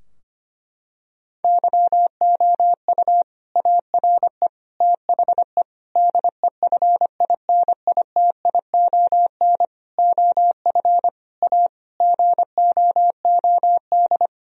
Morse code, heard as a keyed tone, about 25 wpm, 700 Hz. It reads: YOU ARE THE DEFINITION OF A GOOD
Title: Sentences from Top 1000 Words - Review Part 4 - 25wpm